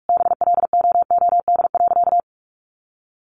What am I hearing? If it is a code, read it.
Morse code, heard as a keyed tone, about 44 wpm, 700 Hz. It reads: BLOOD.
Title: Sentences from Top 1400 Words 45wpm